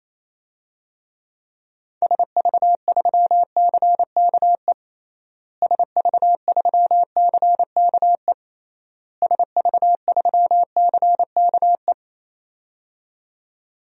Morse code, heard as a keyed tone, about 28 wpm, 700 Hz. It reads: SV3CKE SV3CKE SV3CKE
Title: DXCC-100 Call Signs - 6 Characters - Repeated 3x - 28wpm